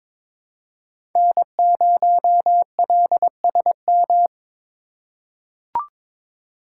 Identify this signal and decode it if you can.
Morse code, heard as a keyed tone, about 22 wpm, 700 Hz. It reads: N0LSM
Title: USA Call Signs - 5 Characters - 4x Character Spacing - 22wpm